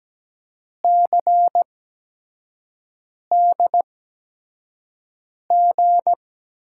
Morse code, heard as a keyed tone, about 17 wpm, spaced as for 5 wpm, 700 Hz. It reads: CDG
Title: USA Call Signs - 8x Character Spacing - 17wpm